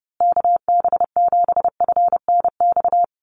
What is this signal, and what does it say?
Morse code, heard as a keyed tone, about 30 wpm, 700 Hz. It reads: KB7FNX
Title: USA Call Signs Repeated 2x 30wpm